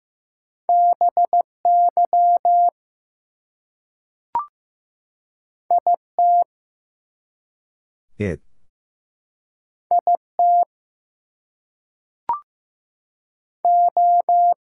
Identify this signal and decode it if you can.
Morse code, heard as a keyed tone, about 15 wpm, 700 Hz. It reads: BY IT IT O
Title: Two Letter Words - 15wpm